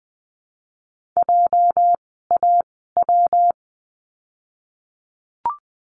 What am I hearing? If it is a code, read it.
Morse code, heard as a keyed tone, about 20 wpm, 700 Hz. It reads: J A W
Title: 206 Lesson 33f - Letter j - Words 20wpm